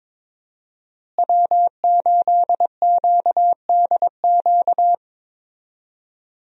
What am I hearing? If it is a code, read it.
Morse code, heard as a keyed tone, about 22 wpm, 700 Hz. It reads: W8QDQ
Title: USA Call Signs - 5 Characters - Repeated 2x - 22wpm